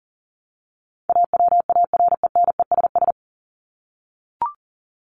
Morse code, heard as a keyed tone, about 39 wpm, 700 Hz. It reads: AWARENESS
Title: Top 1501-1750 Words 40wpm